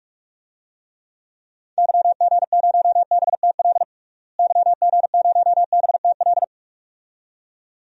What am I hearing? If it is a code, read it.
Morse code, heard as a keyed tone, about 46 wpm, 700 Hz. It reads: YG0BTL YG0BTL
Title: DXCC-100 Call Signs - 6 Characters - Repeated 2x - 45wpm